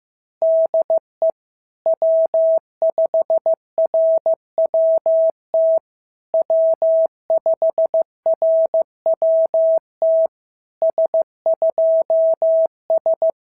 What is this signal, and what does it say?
Morse code, heard as a keyed tone, about 14 wpm, 650 Hz. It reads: DE W5RWT W5RWT S2S
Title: SOTA QSO Practice 15wpm